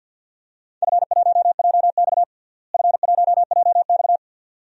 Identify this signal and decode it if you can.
Morse code, heard as a keyed tone, about 52 wpm, 700 Hz. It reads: F1JX F1JX
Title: DXCC-100 Call Signs - 4 Characters - Repeated 3x - 50wpm